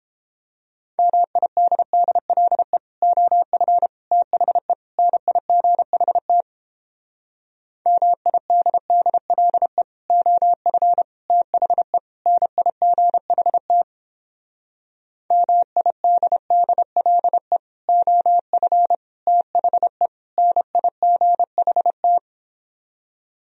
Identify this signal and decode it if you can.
Morse code, heard as a keyed tone, about 31 wpm, 700 Hz. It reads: MIDDLE OF THE NIGHT MIDDLE OF THE NIGHT MIDDLE OF THE NIGHT